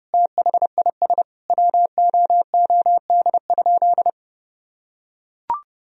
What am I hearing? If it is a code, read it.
Morse code, heard as a keyed tone, about 30 wpm, 700 Hz. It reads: THIS WOOD?
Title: Sentences from Top 1000 Words - Review Part 2 - 30wpm